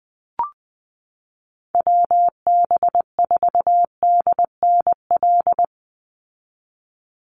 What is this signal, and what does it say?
Morse code, heard as a keyed tone, about 20 wpm, 700 Hz. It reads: WB4DNL
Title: Call Signs Repeated 2x - 20wpm